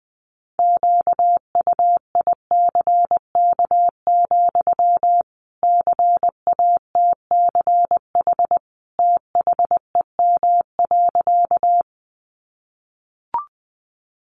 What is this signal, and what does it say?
Morse code, heard as a keyed tone, about 20 wpm, 700 Hz. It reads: QUICK, CATCH THEM.
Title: Sentences from Top 1500 Words 20wpm